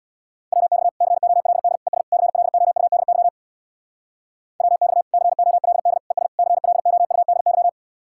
Morse code, heard as a keyed tone, about 66 wpm, 700 Hz. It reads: WX COLD ES CLOUDY WX COLD ES CLOUDY